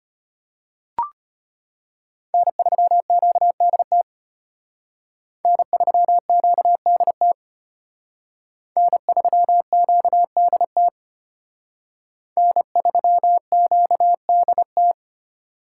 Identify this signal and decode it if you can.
Morse code, heard as a keyed tone, about 32 wpm, 700 Hz. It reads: N3QDT N3QDT N3QDT N3QDT